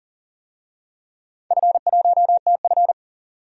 Morse code, heard as a keyed tone, about 40 wpm, 700 Hz. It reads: F1TF
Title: DXCC-100 Call Signs - 4 Characters - 40wpm